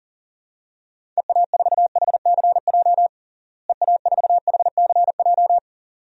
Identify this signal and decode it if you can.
Morse code, heard as a keyed tone, about 40 wpm, 700 Hz. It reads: EA4HCJ EA4HCJ
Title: DXCC-100 Call Signs - 6 Characters - Repeated 2x - 40wpm